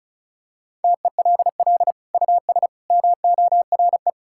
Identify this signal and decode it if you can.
Morse code, heard as a keyed tone, about 34 wpm, 700 Hz. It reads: TELL US MORE
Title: Sentences from Top 1600 Words 35wpm